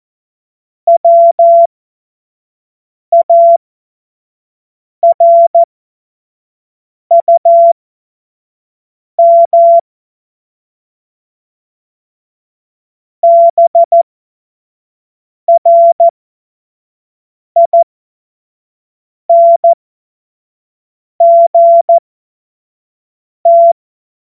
Morse code, heard as a keyed tone, about 14 wpm, spaced as for 5 wpm, 680 Hz. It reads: WARUM BRINGT
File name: Morsekurs / Witze / Witz 16. November 2021, 25 F70